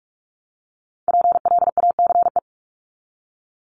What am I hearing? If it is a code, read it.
Morse code, heard as a keyed tone, about 44 wpm, 700 Hz. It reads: PLACE
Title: Top 500 Words 45wpm